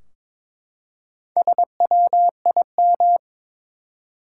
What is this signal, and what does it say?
Morse code, heard as a keyed tone, about 23 wpm, 700 Hz. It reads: SWIM
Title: Four Letter Words - 22wpm